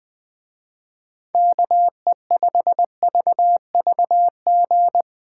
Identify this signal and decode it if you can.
Morse code, heard as a keyed tone, about 20 wpm, 700 Hz. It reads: KE5VVG
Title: Speed Racing - USA 2x3 Call Signs 17wpm